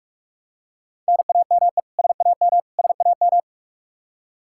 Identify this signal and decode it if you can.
Morse code, heard as a keyed tone, about 41 wpm, 700 Hz. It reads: NAME SAM SAM
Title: Practice QSO - Exchange 1 - Beginning - 45wpm